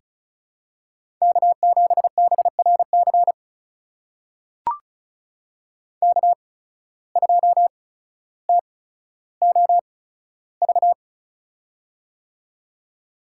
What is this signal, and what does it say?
Morse code, heard as a keyed tone, about 35 wpm, spaced as for 15 wpm, 690 Hz. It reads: K7BRC K2TOV
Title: USA Call Signs - 5 Characters - 8x Character Spacing - 35wpm